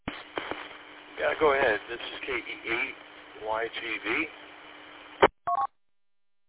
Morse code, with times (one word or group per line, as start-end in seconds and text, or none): none